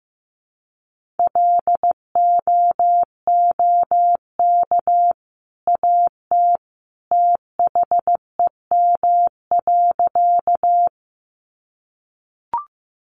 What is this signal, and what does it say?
Morse code, heard as a keyed tone, about 15 wpm, 700 Hz. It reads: LOOK AT THEM.
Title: Sentences from Top 100 Words 15wpm